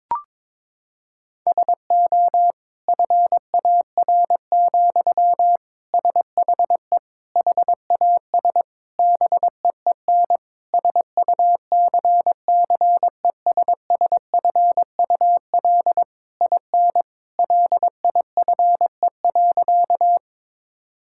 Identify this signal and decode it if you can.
Morse code, heard as a keyed tone, about 22 wpm, 700 Hz. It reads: SO FAR, SHE HAS BEEN SUCCESSFUL IN LIFE.
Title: Sentences from Top 1000 Words - Review Part 6 - 22wpm